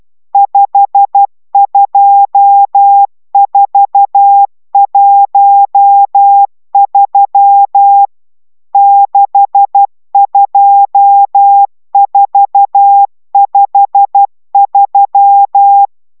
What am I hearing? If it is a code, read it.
Morse code, heard as a keyed tone, about 12 wpm, 800 Hz. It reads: 52413 62453
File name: taviro tanulas / cw 69 22 10